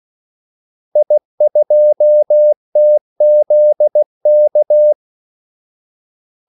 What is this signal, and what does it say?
Morse code, heard as a keyed tone, about 16 wpm, 600 Hz. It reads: I2TZK
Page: Kaldesignaler 16 wpm